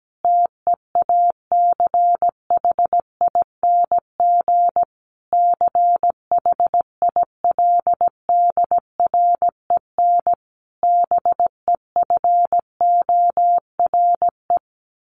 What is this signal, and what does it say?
Morse code, heard as a keyed tone, about 17 wpm, 700 Hz. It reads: TEACHING CHILDREN BEFORE
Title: Sentences from Top 1100 Words 17wpm